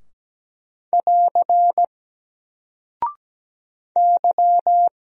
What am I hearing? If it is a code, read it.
Morse code, heard as a keyed tone, about 17 wpm, 700 Hz. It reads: <AR> Y
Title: Single Character (Letter:Number:Punctuation:Prosign) Alt 17wpm